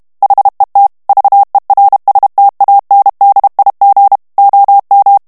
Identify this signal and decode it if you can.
Morse code, heard as a keyed tone, about 32 wpm, 800 Hz. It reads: HET VERSTANDIG OM
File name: morse mp3 / HOLLANDAIS texte clair 2 vitesse 30 mots minute